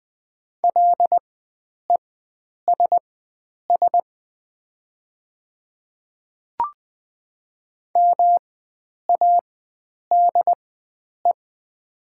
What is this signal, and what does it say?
Morse code, heard as a keyed tone, about 20 wpm, spaced as for 9 wpm, 700 Hz. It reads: LESS MADE
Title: 4x Character Spacing - Top 500 Words - 20wpm